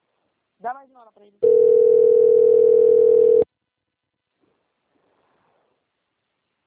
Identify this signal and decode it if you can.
Ringback tone (North America), 440 + 480 Hz, with speech.
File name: out- - - - - .76875